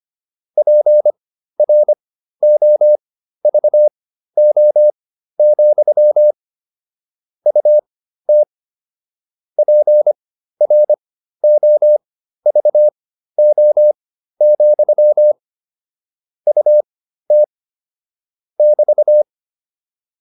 Morse code, heard as a keyed tone, about 25 wpm, spaced as for 13 wpm, 600 Hz. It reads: PROVO, UT PROVO, UT <BT>
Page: intermediate-practice-files